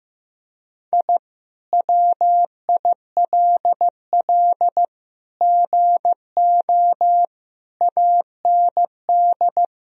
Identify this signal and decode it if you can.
Morse code, heard as a keyed tone, about 14 wpm, 700 Hz. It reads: I WILL GO AND
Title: Sentences from Top 1000 Words - Review Part 1 - 15wpm